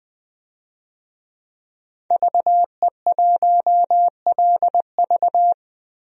Morse code, heard as a keyed tone, about 20 wpm, 700 Hz. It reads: VE1LV
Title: DXCC-100 Call Signs - 5 Characters - 20wpm